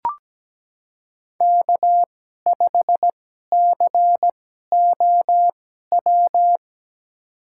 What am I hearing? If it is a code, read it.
Morse code, heard as a keyed tone, about 17 wpm, spaced as for 12 wpm, 700 Hz. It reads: K5COW
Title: USA Call Signs - 2x Character Spacing - 17wpm